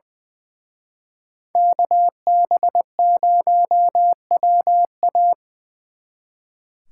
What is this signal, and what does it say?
Morse code, heard as a keyed tone, about 23 wpm, 700 Hz. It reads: KB0WA
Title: Speed Racing - USA 2x2 Call Signs 20wpm